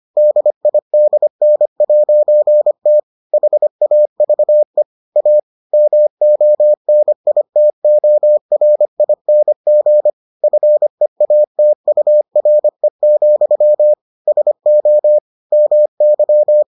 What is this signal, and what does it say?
Morse code, heard as a keyed tone, about 25 wpm, 600 Hz. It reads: DIDN'T HAVE A MONITORING FEATURE, SO MY